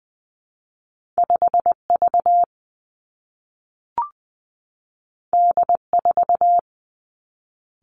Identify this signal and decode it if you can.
Morse code, heard as a keyed tone, about 20 wpm, 700 Hz. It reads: 5 V D 4
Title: Two Challenging Letter-Number Combinations 20wpm